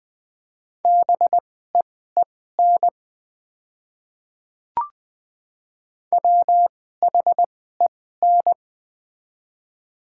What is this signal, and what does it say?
Morse code, heard as a keyed tone, about 20 wpm, spaced as for 14 wpm, 700 Hz. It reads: BEEN WHEN
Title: 2x Character Spacing - Top 100 Words - 20wpm